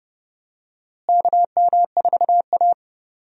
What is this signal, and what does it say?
Morse code, heard as a keyed tone, about 30 wpm, 700 Hz. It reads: KM4A
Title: USA Call Signs - 4 Characters - Repeated 2x - 30wpm